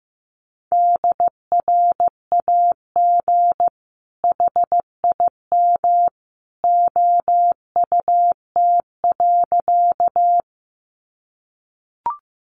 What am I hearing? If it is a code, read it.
Morse code, heard as a keyed tone, about 15 wpm, 700 Hz. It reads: DRAG HIM OUT.
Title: Sentences from Top 1500 Words 15wpm